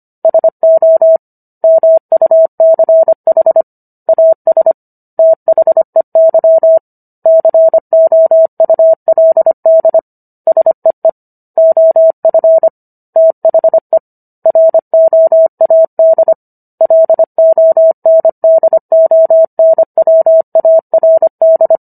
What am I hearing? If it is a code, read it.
Morse code, heard as a keyed tone, about 25 wpm, 650 Hz. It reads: SO MUCH AS THEY COULD SEE OF THE ROAD LONDONWARD